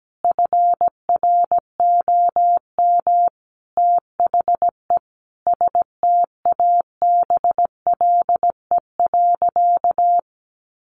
Morse code, heard as a keyed tone, about 17 wpm, 700 Hz. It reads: FROM THE STABLE.
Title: Sentences from Top 1500 Words 17wpm